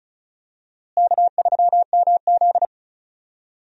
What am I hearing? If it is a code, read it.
Morse code, heard as a keyed tone, about 36 wpm, 700 Hz. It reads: K3MZ
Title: USA Call Signs - 4 Characters - Repeated 2x - 35wpm